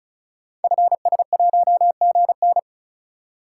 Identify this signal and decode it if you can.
Morse code, heard as a keyed tone, about 36 wpm, 690 Hz. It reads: FS1GN
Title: Code Groups - 5 Letter:Numbers - 35wpm